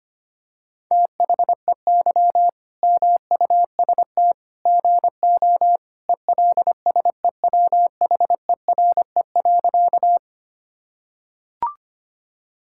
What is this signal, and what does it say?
Morse code, heard as a keyed tone, about 25 wpm, 700 Hz. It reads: THEY MUST GO ELSEWHERE.
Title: Sentences from Top 1600 Words 25wpm